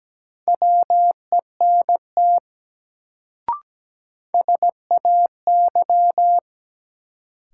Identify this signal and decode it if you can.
Morse code, heard as a keyed tone, about 17 wpm, 700 Hz. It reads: WENT SAY
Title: Sentences from Top 200 Words - New Word Drill - 17wpm